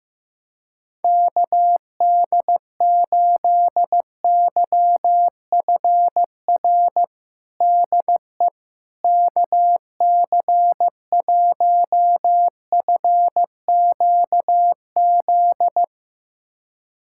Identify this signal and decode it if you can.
Morse code, heard as a keyed tone, about 15 wpm, 700 Hz. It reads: KD8YFR DE KC1FQZ
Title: USA Callsign DE Callsign 15wpm